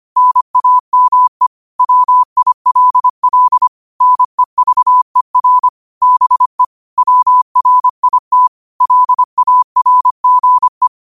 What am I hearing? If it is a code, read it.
Morse code, heard as a keyed tone, about 25 wpm, 1000 Hz. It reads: NAME WILL NEVER BE WRIT LARGE
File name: 25wpm